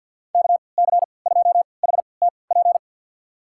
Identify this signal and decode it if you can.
Morse code, heard as a keyed tone, about 51 wpm, spaced as for 28 wpm, 700 Hz. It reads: KC2HTP
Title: USA Call Signs - 6 Characters - 3x Character Spacing - 50wpm